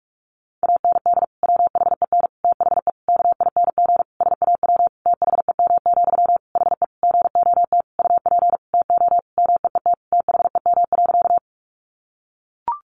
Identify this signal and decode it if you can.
Morse code, heard as a keyed tone, about 44 wpm, 700 Hz. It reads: AND WHEN THE KING SAW THEM, HE GOT UP TO MEET THEM.
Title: Sentences from Top 1100 Words 45wpm